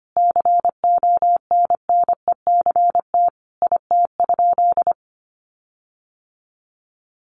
Morse code, heard as a keyed tone, about 25 wpm, 700 Hz. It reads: CONNECT IT?
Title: Sentences from Top 1500 Words 25wpm